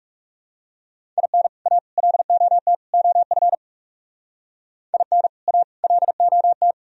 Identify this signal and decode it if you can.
Morse code, heard as a keyed tone, about 36 wpm, 690 Hz. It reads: IN A LOT OF IN A LOT
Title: Speed-Racing - Sets of 4 Words 30wpm